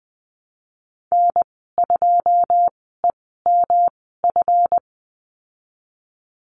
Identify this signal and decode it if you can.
Morse code, heard as a keyed tone, about 20 wpm, spaced as for 15 wpm, 700 Hz. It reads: N2EMF
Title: 173 Lesson 28g - Number 9 - Call Signs 20wpm